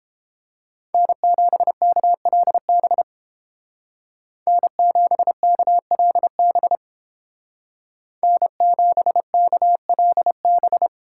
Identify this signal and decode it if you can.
Morse code, heard as a keyed tone, about 29 wpm, 700 Hz. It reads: N7KLB N7KLB N7KLB